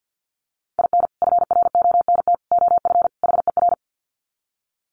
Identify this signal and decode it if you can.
Morse code, heard as a keyed tone, about 50 wpm, 700 Hz. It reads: IN FRONT OF HER
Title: Sets of 4 Words 50wpm